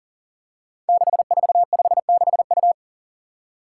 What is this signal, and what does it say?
Morse code, heard as a keyed tone, about 38 wpm, 700 Hz. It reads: 6456U
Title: Five Challenging Letter-Number Combinations 40wpm